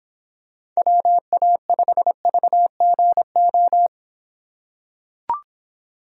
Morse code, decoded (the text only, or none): WA5VGO